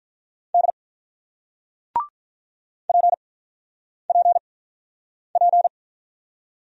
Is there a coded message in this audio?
Yes: Morse code from 0.5 to 5.7 s, heard as a keyed tone, about 46 wpm, 690 Hz. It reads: D P P P